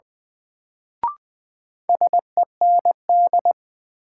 Morse code, heard as a keyed tone, about 21 wpm, 700 Hz. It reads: SEND
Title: Sentences from Top 1500 Words - New Word Drill Alt - 20wpm